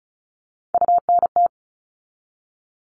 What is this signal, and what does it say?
Morse code, heard as a keyed tone, about 34 wpm, 700 Hz. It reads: UNT
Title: All Cut Numbers - 3-Digit 35wpm